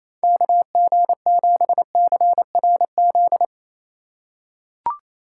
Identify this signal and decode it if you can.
Morse code, heard as a keyed tone, about 28 wpm, 700 Hz. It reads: KG7CRZ
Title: USA Call Signs - 6 Characters - 2x Character Spacing - 28wpm